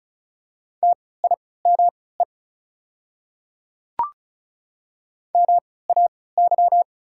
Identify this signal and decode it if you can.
Morse code, heard as a keyed tone, about 36 wpm, spaced as for 20 wpm, 690 Hz. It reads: TIME MAY
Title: 3x Character Spacing - Top 100 Words - 35wpm